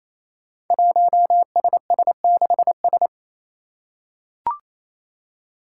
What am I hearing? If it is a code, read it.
Morse code, heard as a keyed tone, about 27 wpm, 700 Hz. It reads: 1SS6S
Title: Five Challenging Letter-Number Combinations 28wpm